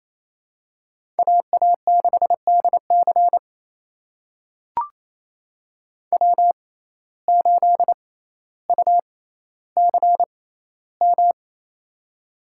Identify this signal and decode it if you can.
Morse code, heard as a keyed tone, about 28 wpm, spaced as for 14 wpm, 700 Hz. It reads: AA6DC W8UCM
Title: USA Call Signs - 5 Characters - 6x Character Spacing - 28wpm